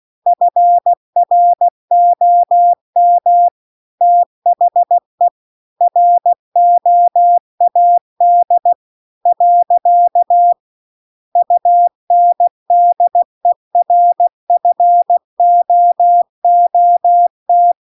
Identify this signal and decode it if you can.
Morse code, heard as a keyed tone, about 16 wpm, 700 Hz. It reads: FROM THE ROAD . UNDERFOOT